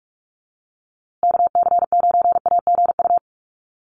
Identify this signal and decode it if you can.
Morse code, heard as a keyed tone, about 44 wpm, 700 Hz. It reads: KC9AGU
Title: USA Call Signs Repeated 2x 45wpm